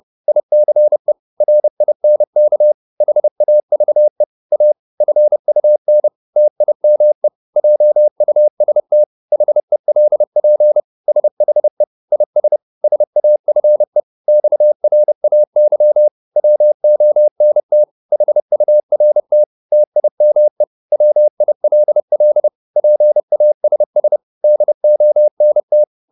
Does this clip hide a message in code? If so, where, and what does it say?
Yes: Morse code from 0.3 to 25.8 s, heard as a keyed tone, about 30 wpm, 600 Hz. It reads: ICE RINK HAVE A FUN TIME JUST HELP SHE IS SAFE XRAY WONT HURT TIME WILL PASS DONT